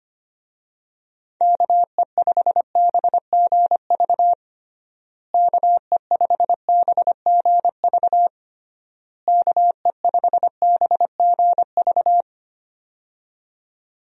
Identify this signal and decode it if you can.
Morse code, heard as a keyed tone, about 25 wpm, 700 Hz. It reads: KE5BGV KE5BGV KE5BGV
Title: USA Call Signs - 6 Characters - Repeated 3x - 25wpm